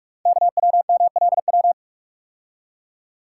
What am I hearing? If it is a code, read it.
Morse code, heard as a keyed tone, about 44 wpm, 700 Hz. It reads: KWMLW
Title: Code Groups - 5 Letters - 45wpm